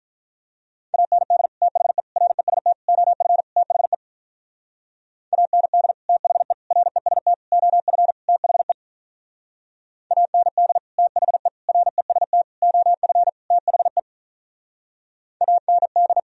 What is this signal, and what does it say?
Morse code, heard as a keyed tone, about 46 wpm, 700 Hz. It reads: AND THE REST OF THE AND THE REST OF THE AND THE REST OF THE AND